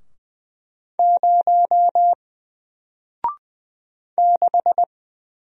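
Morse code, heard as a keyed tone, about 20 wpm, 700 Hz. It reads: 0 6
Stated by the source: Single letter and number Morse drill at 20 wpm, from N4LCD.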